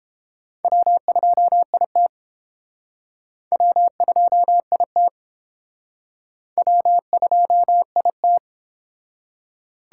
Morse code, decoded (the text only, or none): W2IT W2IT W2IT